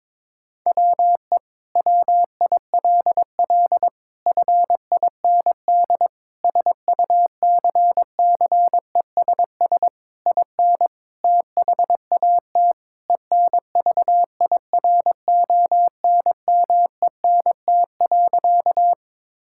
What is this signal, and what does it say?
Morse code, heard as a keyed tone, about 22 wpm, 700 Hz. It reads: WE WILL FIND SUCCESS IN THAT ENVIRONMENT.
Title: Sentences from Top 1000 Words - Review Part 5 - 22wpm